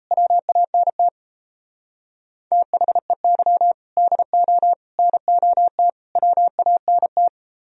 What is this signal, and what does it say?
Morse code, heard as a keyed tone, about 35 wpm, 700 Hz. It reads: WANT THEY DO NOT WANT
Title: Speed-Racing - Sets of 4 Words 28wpm